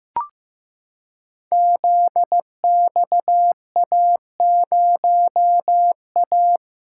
Morse code, heard as a keyed tone, about 15 wpm, 700 Hz. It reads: ZXA0A
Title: Code Groups - 5 Letter:Numbers - 15wpm